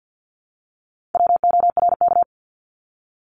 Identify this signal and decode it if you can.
Morse code, heard as a keyed tone, about 50 wpm, 700 Hz. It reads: WORK
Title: ING Suffix - Top 100 Words with Base Words 50wpm